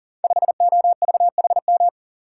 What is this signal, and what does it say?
Morse code, heard as a keyed tone, about 42 wpm, 700 Hz. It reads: 5OVHM
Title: Code Groups - 5 Letter:Numbers - 40wpm